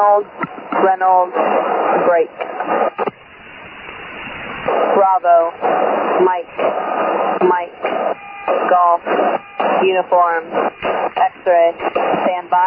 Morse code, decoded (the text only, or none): none